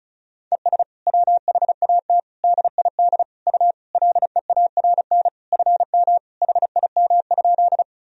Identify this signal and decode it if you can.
Morse code, heard as a keyed tone, about 36 wpm, 700 Hz. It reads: ES WHAT DID U LEARN FM HIM?
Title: Sentences from Top 300 Words and Abbreviations 35wpm